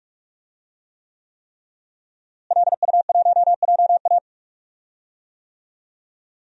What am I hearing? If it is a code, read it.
Morse code, heard as a keyed tone, about 46 wpm, 700 Hz. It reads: LU1JA
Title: DXCC-100 Call Signs - 5 Characters - Repeated 3x - 45wpm